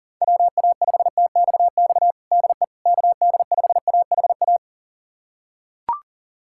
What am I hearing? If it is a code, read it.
Morse code, heard as a keyed tone, about 40 wpm, 700 Hz. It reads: WA5TXX DE KD5AHA
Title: USA Callsign DE Callsign - 6 Characters - 40wpm